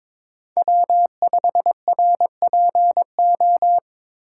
Morse code, heard as a keyed tone, about 21 wpm, 700 Hz. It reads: W5RPO
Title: USA Call Signs - 5 Characters - 4x Character Spacing - 22wpm